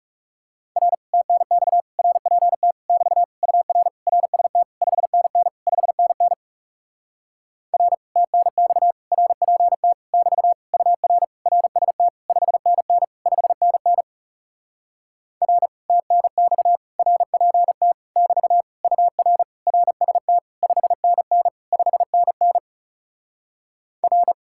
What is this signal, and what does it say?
Morse code, heard as a keyed tone, about 40 wpm, 700 Hz. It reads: R TNX RPT <BT> UR RST 5NN 5NN R TNX RPT <BT> UR RST 5NN 5NN R TNX RPT <BT> UR RST 5NN 5NN R